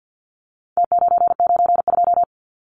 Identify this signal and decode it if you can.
Morse code, heard as a keyed tone, about 50 wpm, 700 Hz. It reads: T992
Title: Numbers - Contest Style - 4-Digit 50wpm